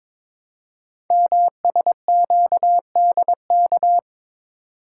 Morse code, heard as a keyed tone, about 22 wpm, 700 Hz. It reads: MSQDK
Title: Code Groups - 5 Letters - 22wpm